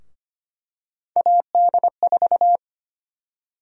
Morse code, heard as a keyed tone, about 25 wpm, 700 Hz. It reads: AD4
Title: USA Call Sign Prefixes - Repeated 2x - 25wpm